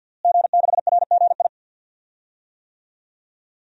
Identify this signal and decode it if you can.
Morse code, heard as a keyed tone, about 48 wpm, 700 Hz. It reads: G6RGI
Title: DXCC-100 Call Signs - 5 Characters - Repeated 2x - 50wpm